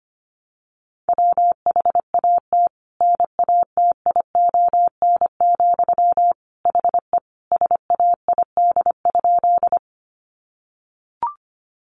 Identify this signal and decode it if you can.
Morse code, heard as a keyed tone, about 25 wpm, 700 Hz. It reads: WHAT NATION, HE SAID?
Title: Sentences from Top 900 Words 25wpm